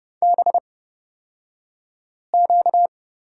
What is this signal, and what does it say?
Morse code, heard as a keyed tone, about 30 wpm, spaced as for 5 wpm, 700 Hz. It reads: BQ
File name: Lesson 38d - Letter q - Two Characters 30wpm